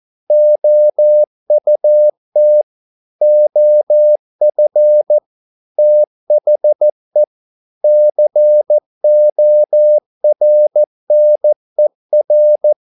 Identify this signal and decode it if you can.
Morse code, heard as a keyed tone, about 14 wpm, 600 Hz. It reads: OUT OF THE CORNER